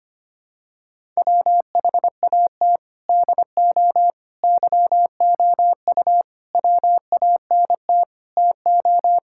Morse code, heard as a keyed tone, about 25 wpm, 700 Hz. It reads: WHAT DO YOU WANT TO
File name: practice / 25wpm / Sentences with 3-Word Phrases - 25wpm